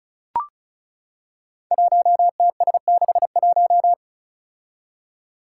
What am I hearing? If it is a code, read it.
Morse code, heard as a keyed tone, about 36 wpm, 700 Hz. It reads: 1TS61
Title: Five Challenging Letter-Number Combinations 35wpm